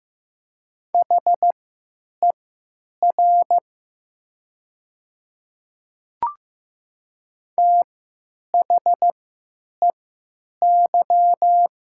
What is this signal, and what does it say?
Morse code, heard as a keyed tone, about 15 wpm, spaced as for 9 wpm, 700 Hz. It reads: HER THEY
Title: 3x Character Spacing - Top 100 Words - 15wpm